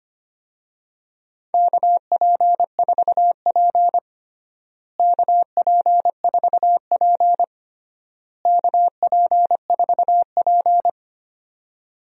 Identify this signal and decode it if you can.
Morse code, heard as a keyed tone, about 25 wpm, 700 Hz. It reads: KP4P KP4P KP4P
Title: DXCC-100 Call Signs - 4 Characters - Repeated 3x - 25wpm